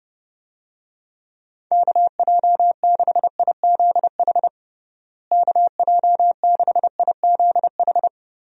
Morse code, heard as a keyed tone, about 32 wpm, 700 Hz. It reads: KJ6IZH KJ6IZH
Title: USA Call Signs - 6 Characters - Repeated 2x - 30wpm